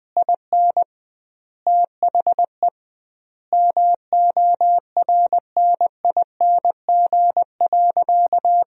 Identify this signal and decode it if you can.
Morse code, heard as a keyed tone, about 20 wpm, 700 Hz. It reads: IN THE MORNING.
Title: 2x Word-Spacing - Sentences from Top 400 Words - 20wpm